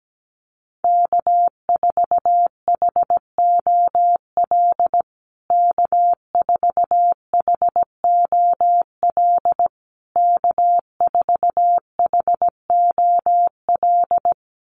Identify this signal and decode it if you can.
Morse code, heard as a keyed tone, about 17 wpm, 700 Hz. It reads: K4HOL K4HOL K4HOL
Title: USA Call Signs Repeated 3x 17wpm